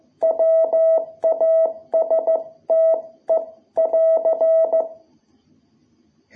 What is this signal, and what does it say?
Morse code, heard as a keyed tone, about 15 wpm, 660 Hz. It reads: WASTE <AR>
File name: jh8cbh / 2021cw